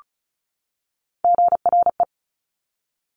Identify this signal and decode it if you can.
Morse code, heard as a keyed tone, about 36 wpm, 700 Hz. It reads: GRE
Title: Top Three Letter Combinations in Words 35wpm